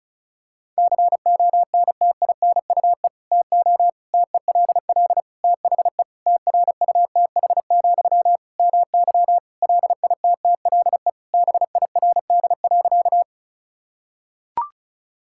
Morse code, heard as a keyed tone, about 36 wpm, 700 Hz. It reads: CONTINUE TO TELL THE TRUTH, MY LITTLE BIRD.
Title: Sentences from Top 1000 Words - Review Part 6 - 35wpm